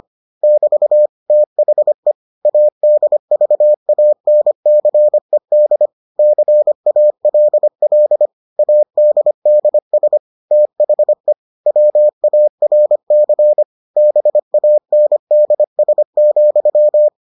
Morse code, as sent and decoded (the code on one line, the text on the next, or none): -...- / - .... / .- -.. ...- .- -. -.-. -.. / -.-. .- .-.. .-.. / .- -.. -.. ... / - .... / .-- .- .-. -.-. / -... .- -. -.. ... --..--
<BT> THE ADVANCED CALL ADDS THE WARC BANDS,